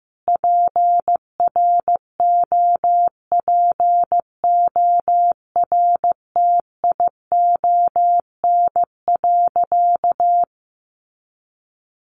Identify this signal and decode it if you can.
Morse code, heard as a keyed tone, about 15 wpm, 700 Hz. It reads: PROPORTION.
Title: A Sound Of Thunder 15wpm